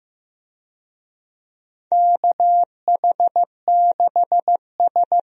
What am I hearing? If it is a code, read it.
Morse code, heard as a keyed tone, about 15 wpm, 700 Hz. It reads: KH6S
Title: DXCC-100 Call Signs - 4 Characters - Repeated 3x - 15wpm